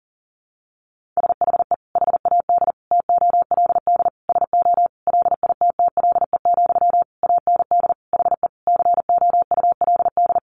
Morse code, heard as a keyed tone, about 40 wpm, 700 Hz. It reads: SHE HAD TOLD SO LITTLE, AND HE COULD
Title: Sentences from Top 1500 Words 40wpm